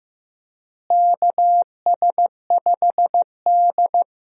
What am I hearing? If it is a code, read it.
Morse code, heard as a keyed tone, about 16 wpm, 700 Hz. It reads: KS5D
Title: USA Call Signs - 4 Characters - Repeated 3x - 15wpm